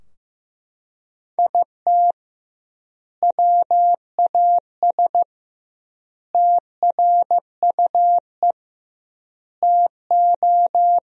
Morse code, heard as a keyed tone, about 15 wpm, 700 Hz. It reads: IT WAS TRUE TO
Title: 2x Word-Spacing - Sentences from Top 400 Words - 15wpm